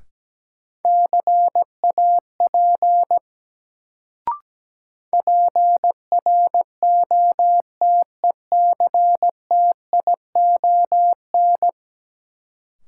Morse code, heard as a keyed tone, about 17 wpm, 700 Hz. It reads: CAP PROTECTION
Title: Sentences from Top 1600 Words - New Word Drill Alt - 17wpm